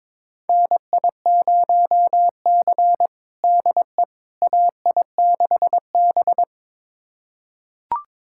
Morse code, heard as a keyed tone, about 22 wpm, 700 Hz. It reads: NI0C DE AI6B
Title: USA Callsign DE Callsign - 4 Characters - 22wpm